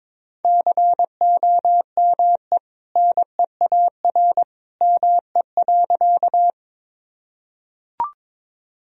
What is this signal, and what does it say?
Morse code, heard as a keyed tone, about 22 wpm, 700 Hz. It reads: COME NEAR ME.
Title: Sentences from Top 1000 Words - Review Part 1 - 22wpm